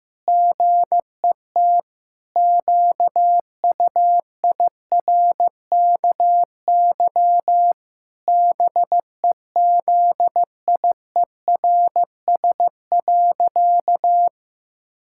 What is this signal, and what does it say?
Morse code, heard as a keyed tone, about 14 wpm, 700 Hz. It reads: GET QUIRKY BEZIERS.
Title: Pangram Challenge v2 - 15wpm